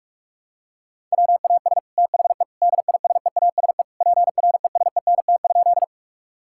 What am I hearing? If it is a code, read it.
Morse code, heard as a keyed tone, about 46 wpm, 700 Hz. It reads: WAS THE DISEASE PRESENT?
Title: Sentences from Top 1000 Words - Review Part 6 - 45wpm